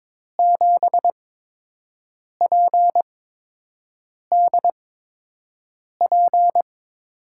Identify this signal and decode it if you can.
Morse code, heard as a keyed tone, about 22 wpm, spaced as for 6 wpm, 700 Hz. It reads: 7PDP
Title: USA Call Signs - 5 Characters - 8x Character Spacing - 22wpm